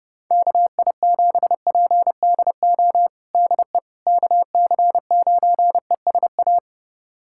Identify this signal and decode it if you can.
Morse code, heard as a keyed tone, about 32 wpm, 700 Hz. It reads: KI7PDO DE KC9ESA
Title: USA Callsign DE Callsign - 6 Characters - 30wpm